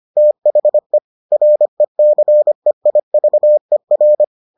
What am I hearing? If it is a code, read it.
Morse code, heard as a keyed tone, about 25 wpm, 600 Hz. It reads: THE RECEIVER